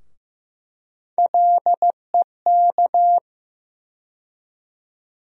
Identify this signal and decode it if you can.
Morse code, heard as a keyed tone, about 15 wpm, 700 Hz. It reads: LEK
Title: USA Call Sign Postfixes - 3 Letters 15wpm